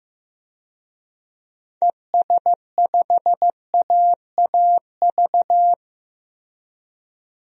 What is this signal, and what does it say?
Morse code, heard as a keyed tone, about 15 wpm, 700 Hz. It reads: ES5AAV
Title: DXCC-100 Call Signs - 6 Characters - Repeated 3x - 15wpm